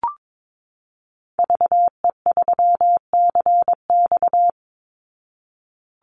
Morse code, heard as a keyed tone, about 22 wpm, 700 Hz. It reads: VE3CX
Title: Call Signs Repeated 2x - 22wpm